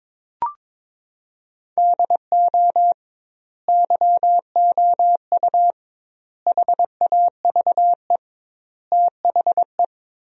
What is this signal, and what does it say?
Morse code, heard as a keyed tone, about 22 wpm, 700 Hz. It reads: DO YOU HAVE THE
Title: 2x Word-Spacing - Sentences from Top 300 Words - 22wpm